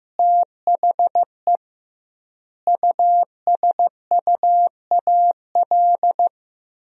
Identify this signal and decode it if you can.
Morse code, heard as a keyed tone, about 15 wpm, 700 Hz. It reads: THE USUAL
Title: 2x Word-Spacing - Sentences from Top 400 Words - 15wpm